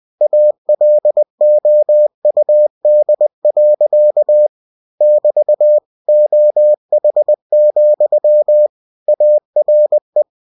Morse code, heard as a keyed tone, about 20 wpm, 600 Hz. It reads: ALOUD. <BT> OH, ARE
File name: cw-books / winnie-the-pooh / winnie-20wpm-page-0098